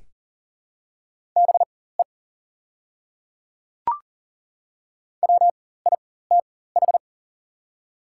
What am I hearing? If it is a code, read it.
Morse code, heard as a keyed tone, about 39 wpm, spaced as for 19 wpm, 700 Hz. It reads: BE WITH
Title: 4x Character Spacing - Top 200 Words - 40wpm